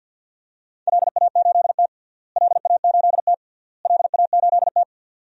Morse code, heard as a keyed tone, about 48 wpm, 700 Hz. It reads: LA8T LA8T LA8T
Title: DXCC-100 Call Signs - 4 Characters - Repeated 3x - 50wpm